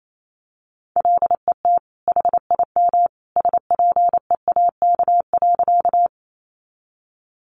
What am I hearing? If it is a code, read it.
Morse code, heard as a keyed tone, about 28 wpm, 700 Hz. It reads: LET HIM SPEAK.
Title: Sentences from Top 1100 Words 28wpm